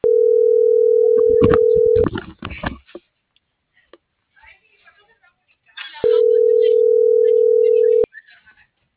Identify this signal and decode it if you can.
Ringback tone (North America), 440 + 480 Hz.